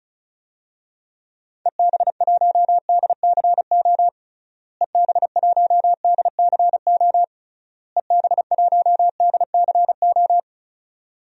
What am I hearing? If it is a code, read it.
Morse code, heard as a keyed tone, about 36 wpm, 700 Hz. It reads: EB1DCO EB1DCO EB1DCO
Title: DXCC-100 Call Signs - 6 Characters - Repeated 3x - 35wpm